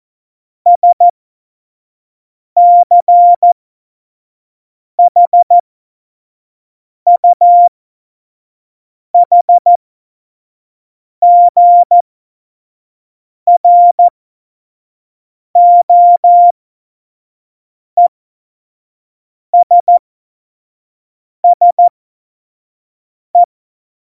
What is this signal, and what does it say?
Morse code, heard as a keyed tone, about 14 wpm, spaced as for 5 wpm, 700 Hz. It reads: SCHUHGROESSE
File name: Morsekurs / Witze / Witz 18. Februar 2020, 25 F70